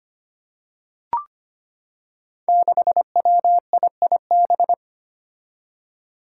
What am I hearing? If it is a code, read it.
Morse code, heard as a keyed tone, about 25 wpm, 700 Hz. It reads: 6WIIB
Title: Five Challenging Letter-Number Combinations 25wpm